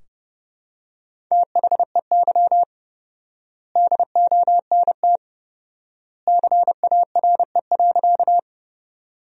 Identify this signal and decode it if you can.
Morse code, heard as a keyed tone, about 30 wpm, 700 Hz. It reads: THEY DONT CARE.
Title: 4x Word-Spacing - Sentences from Top 300 Words - 30wpm